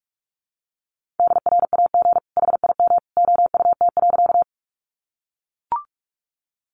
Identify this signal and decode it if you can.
Morse code, heard as a keyed tone, about 44 wpm, 700 Hz. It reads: DRAG HIM OUT.
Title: Sentences from Top 1500 Words 45wpm